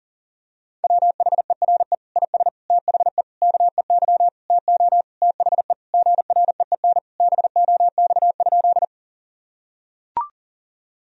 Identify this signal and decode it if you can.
Morse code, heard as a keyed tone, about 40 wpm, 700 Hz. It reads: WHERE IS THE KEY TO THE GREEN BOX?
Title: Sentences from Top 1000 Words - Review Part 4 - 40wpm